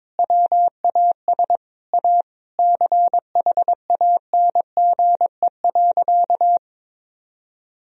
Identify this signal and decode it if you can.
Morse code, heard as a keyed tone, about 22 wpm, 700 Hz. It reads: WAS A CHANGE.
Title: Sentences from Top 1000 Words - Review Part 1 - 22wpm